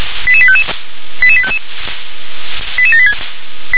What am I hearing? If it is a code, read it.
Five-tone selective call (ZVEI), heard as quick runs of five tones, tones of about 70 ms. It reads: 88855 88855 88786